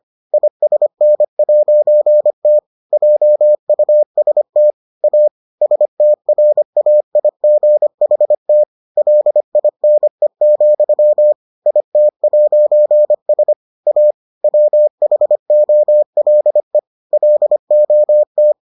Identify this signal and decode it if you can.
Morse code, heard as a keyed tone, about 25 wpm, 600 Hz. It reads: ISN'T JUST A STRAIGHT LINE, IT'S A WHOLE LOT